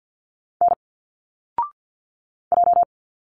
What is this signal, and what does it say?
Morse code, heard as a keyed tone, about 51 wpm, 700 Hz. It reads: N J
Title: Single Letters 50wpm